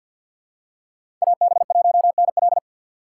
Morse code, heard as a keyed tone, about 51 wpm, 700 Hz. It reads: AB1NL
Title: USA Call Signs - 5 Characters - 3x Character Spacing - 50wpm